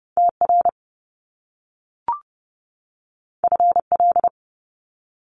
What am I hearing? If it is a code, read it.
Morse code, heard as a keyed tone, about 30 wpm, 700 Hz. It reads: TR FL
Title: Top Two Letter Combinations in Words 30wpm